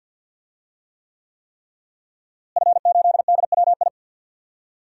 Morse code, heard as a keyed tone, about 52 wpm, 700 Hz. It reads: F8DWI
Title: DXCC-100 Call Signs - 5 Characters - Repeated 3x - 50wpm